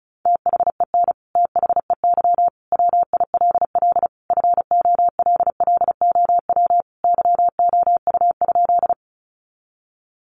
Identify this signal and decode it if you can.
Morse code, heard as a keyed tone, about 34 wpm, 700 Hz. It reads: THEN THEY WILL FOLLOW YOU?
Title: Sentences from Top 200 Words 35wpm